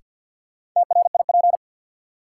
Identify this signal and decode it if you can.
Morse code, heard as a keyed tone, about 53 wpm, 700 Hz. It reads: TRIP
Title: Sentences from Top 1000 Words - Review Part 4 - New Word Drill - 50wpm